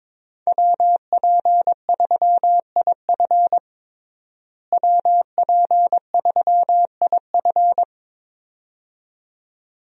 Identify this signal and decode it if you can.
Morse code, heard as a keyed tone, about 22 wpm, 700 Hz. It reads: WP3IF WP3IF
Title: USA Call Signs - 5 Characters - Repeated 2x - 22wpm